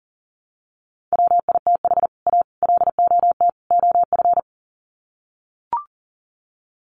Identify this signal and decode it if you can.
Morse code, heard as a keyed tone, about 40 wpm, 700 Hz. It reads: WITH A LOT OF
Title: Sets of 4 Words 40wpm